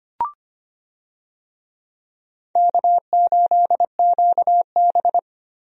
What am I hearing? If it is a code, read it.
Morse code, heard as a keyed tone, about 25 wpm, 700 Hz. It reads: K8QB
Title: USA Call Signs - 4 Characters - Repeated 3x - 25wpm